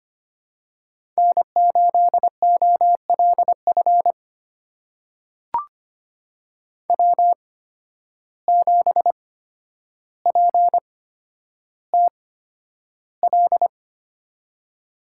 Morse code, heard as a keyed tone, about 25 wpm, spaced as for 11 wpm, 700 Hz. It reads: N8OLF W7PTL
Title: USA Call Signs - 5 Characters - 8x Character Spacing - 25wpm